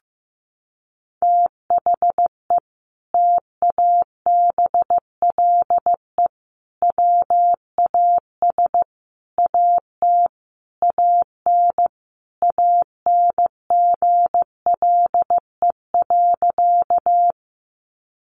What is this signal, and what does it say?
Morse code, heard as a keyed tone, about 15 wpm, 700 Hz. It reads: THE TABLE WAS AT AN ANGLE.
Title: Sentences from Top 1500 Words 15wpm